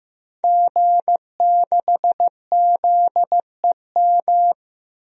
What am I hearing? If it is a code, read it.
Morse code, heard as a keyed tone, about 15 wpm, 700 Hz. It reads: G6ZEM
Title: DXCC-100 Call Signs - 5 Characters - Repeated 3x - 15wpm